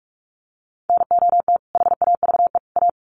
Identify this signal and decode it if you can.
Morse code, heard as a keyed tone, about 44 wpm, 700 Hz. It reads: NOT HAVE A
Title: Sets of 3 Words 45wpm